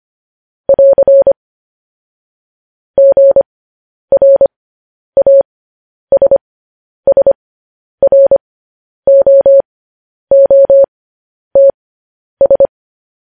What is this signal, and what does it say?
Morse code, heard as a keyed tone, about 25 wpm, spaced as for 10 wpm, 550 Hz. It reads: <AR> GRASSROOTS
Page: news-headlines-in-morse-code-at-10-wpm